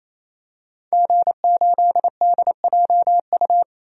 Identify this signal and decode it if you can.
Morse code, heard as a keyed tone, about 28 wpm, 700 Hz. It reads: G8DJU
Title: DXCC-100 Call Signs - 5 Characters - Repeated 3x - 28wpm